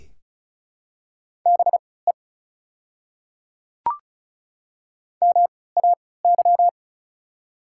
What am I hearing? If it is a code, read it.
Morse code, heard as a keyed tone, about 36 wpm, spaced as for 20 wpm, 690 Hz. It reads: BE MAY
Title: 3x Character Spacing - Top 100 Words - 35wpm